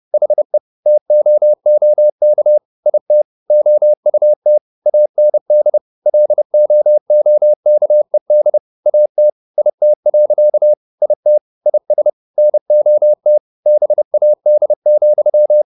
Morse code, heard as a keyed tone, about 30 wpm, 600 Hz. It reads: HE TOOK IT OUT AND LOOKED AT IT. IT IS NOT BAD,